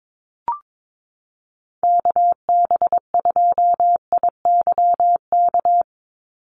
Morse code, heard as a keyed tone, about 22 wpm, 700 Hz. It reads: KB2IYK
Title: USA Call Signs 22wpm